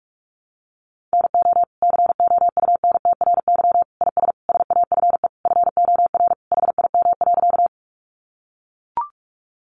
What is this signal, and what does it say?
Morse code, heard as a keyed tone, about 44 wpm, 700 Hz. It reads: NO COUNTRY IS SAFE FOR HIM.
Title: Sentences from Top 900 Words 45wpm